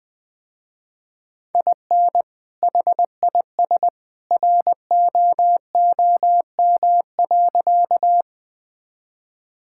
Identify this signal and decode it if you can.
Morse code, heard as a keyed tone, about 20 wpm, 700 Hz. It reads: IN HIS ROOM.
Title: Sentences with 2-Word Phrases - Spoken Beginning - 20wpm